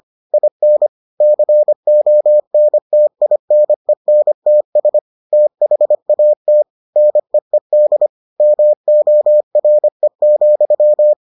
Morse code, heard as a keyed tone, about 25 wpm, 600 Hz. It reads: IN CONTINENTS THAT NEED MORE,